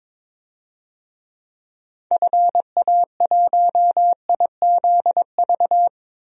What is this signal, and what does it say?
Morse code, heard as a keyed tone, about 22 wpm, 700 Hz. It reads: FA1IZV
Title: DXCC-100 Call Signs - 6 Characters - Repeated 3x - 22wpm